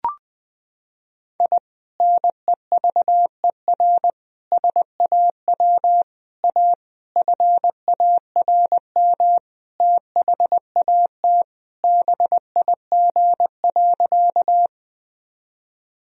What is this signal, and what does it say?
Morse code, heard as a keyed tone, about 20 wpm, 700 Hz. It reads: I NEVER SAW A FARM THAT BIG.
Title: Sentences from Top 1000 Words - Review Part 2 - 20wpm